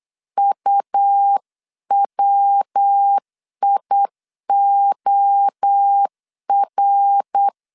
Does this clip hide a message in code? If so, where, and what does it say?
Yes: Morse code from 0.4 to 7.5 s, heard as a keyed tone, about 8 wpm, 800 Hz. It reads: UWIOR